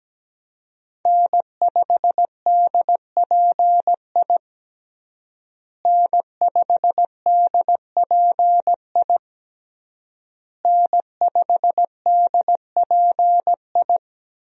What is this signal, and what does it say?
Morse code, heard as a keyed tone, about 17 wpm, 700 Hz. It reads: N5DPI N5DPI N5DPI
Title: USA Call Signs - 5 Characters - Repeated 3x - 17wpm